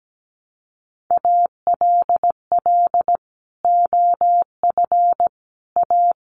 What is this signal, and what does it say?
Morse code, heard as a keyed tone, about 17 wpm, 700 Hz. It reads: ALL OF A
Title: Sets of 4 Words 17wpm